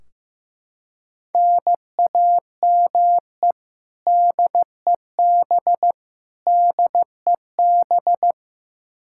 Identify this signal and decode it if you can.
Morse code, heard as a keyed tone, about 15 wpm, 700 Hz. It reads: NAME DEB DEB
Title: Practice QSO - Exchange 1 - Beginning - 15wpm